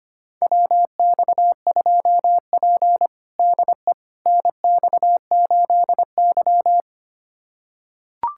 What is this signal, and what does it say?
Morse code, heard as a keyed tone, about 25 wpm, 700 Hz. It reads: WX2P DE NX8Y
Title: USA Callsign DE Callsign - 4 Characters - 25wpm